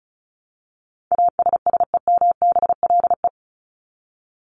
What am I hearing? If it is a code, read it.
Morse code, heard as a keyed tone, about 34 wpm, 700 Hz. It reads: ASSEMBLE
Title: LY Suffix - Top 100 Words with Base Words 35wpm